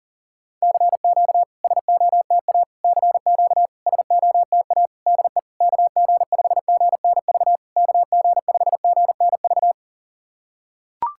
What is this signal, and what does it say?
Morse code, heard as a keyed tone, about 40 wpm, 690 Hz. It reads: CQ SOTA CQ SOTA DE KG5GNV KG5GNV
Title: SOTA QSO Practice 40wpm